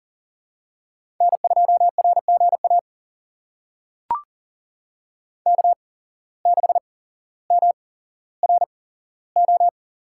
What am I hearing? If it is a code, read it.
Morse code, heard as a keyed tone, about 40 wpm, spaced as for 17 wpm, 700 Hz. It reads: N2RGA K6MRO
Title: USA Call Signs - 8x Character Spacing - 40wpm